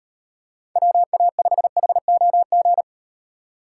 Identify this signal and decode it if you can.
Morse code, heard as a keyed tone, about 38 wpm, 700 Hz. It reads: WA5HOG